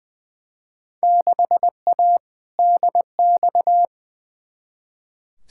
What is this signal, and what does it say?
Morse code, heard as a keyed tone, about 20 wpm, 700 Hz. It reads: 6A DX
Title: Field Day Exchanges Encore - 20wpm